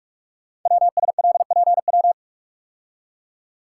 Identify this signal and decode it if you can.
Morse code, heard as a keyed tone, about 44 wpm, 700 Hz. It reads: WSLPW
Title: Code Groups - 5 Letters - 45wpm